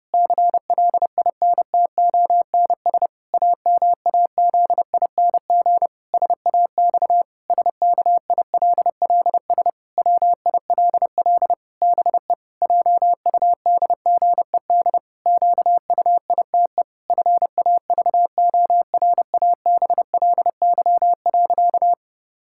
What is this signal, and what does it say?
Morse code, heard as a keyed tone, about 30 wpm, 700 Hz. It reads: CLINTONS AMAZING SAX SKILLS WILL BE JUDGED QUITE FAVORABLY.
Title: Pangram Challenge v2 - 30wpm